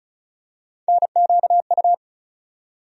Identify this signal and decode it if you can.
Morse code, heard as a keyed tone, about 34 wpm, 700 Hz. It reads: NQU
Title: USA Call Sign Postfixes - 3 Letters 35wpm